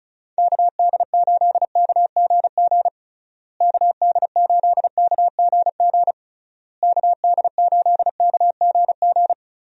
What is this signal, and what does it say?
Morse code, heard as a keyed tone, about 34 wpm, 700 Hz. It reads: KD8KGG KD8KGG KD8KGG
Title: USA Call Signs - 6 Characters - Repeated 3x - 35wpm